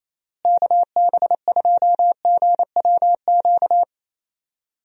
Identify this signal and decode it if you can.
Morse code, heard as a keyed tone, about 27 wpm, 700 Hz. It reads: KB2GWQ
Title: USA Call Signs - 6 Characters - Repeated 3x - 28wpm